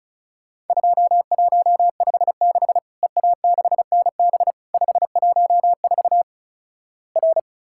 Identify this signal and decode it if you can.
Morse code, heard as a keyed tone, about 34 wpm, 700 Hz. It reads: 2156 EA6NB 514 R
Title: Worked All Europe DX Contest - QTC Exchange - 35wpm